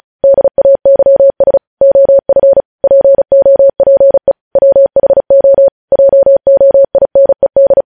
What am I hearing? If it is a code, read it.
Morse code, heard as a keyed tone, about 36 wpm, 550 Hz. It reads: DAYS OF POPE WHO JOINED
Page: news headlines in morse code at 35 wpm